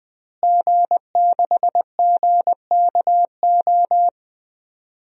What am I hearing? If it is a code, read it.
Morse code, heard as a keyed tone, about 20 wpm, 700 Hz. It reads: G6GKO
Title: DXCC-100 Call Signs - 5 Characters - Repeated 3x - 20wpm